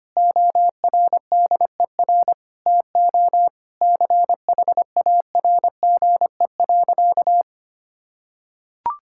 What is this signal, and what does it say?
Morse code, heard as a keyed tone, about 25 wpm, 700 Hz. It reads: ORDER TO CHARGE.
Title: Sentences from Top 1000 Words - Review Part 6 - 25wpm